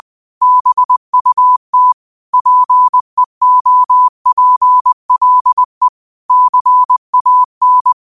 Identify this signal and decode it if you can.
Morse code, heard as a keyed tone, about 20 wpm, 1000 Hz. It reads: BUT PEOPLE CAN
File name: qotdfiles / 20wpm